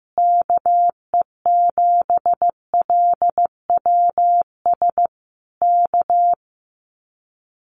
Morse code, heard as a keyed tone, about 15 wpm, 700 Hz. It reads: KE7LWS K
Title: Common QSO Phrases Encore 15wpm